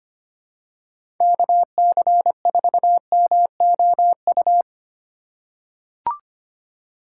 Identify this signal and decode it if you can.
Morse code, heard as a keyed tone, about 25 wpm, 700 Hz. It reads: KC4MOU
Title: USA Call Signs - 6 Characters - 4x Character Spacing - 25wpm